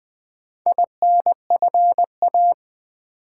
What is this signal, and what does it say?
Morse code, heard as a keyed tone, about 20 wpm, 700 Hz. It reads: INFA
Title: Build A Word - 2 Letters At A Time - 6-Letter Words - 20wpm